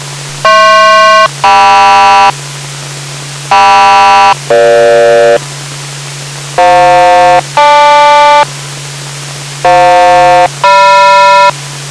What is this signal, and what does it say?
Aircraft SELCAL, heard as two pairs of tones. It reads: HP-KM KM-EG FJ-GL FJ-GM